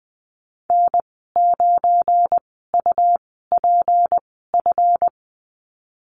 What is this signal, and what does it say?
Morse code, heard as a keyed tone, about 20 wpm, spaced as for 15 wpm, 700 Hz. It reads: N9UPF
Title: 179 Lesson 29g - Slash - Call Signs 20wpm